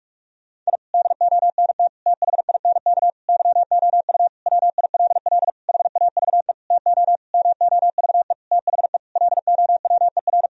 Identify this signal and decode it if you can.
Morse code, heard as a keyed tone, about 41 wpm, 700 Hz. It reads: I DONT THINK YOU WILL HAVE TO MOVE THE LOWER
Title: Sentences from Top 1000 Words - Review Part 4 - 45wpm